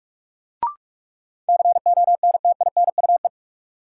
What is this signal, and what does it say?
Morse code, heard as a keyed tone, about 46 wpm, 700 Hz. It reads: CONTINUE
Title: Sentences from Top 1100 Words - New Word Drill Alt - 45wpm